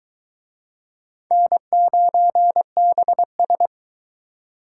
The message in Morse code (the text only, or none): N9BS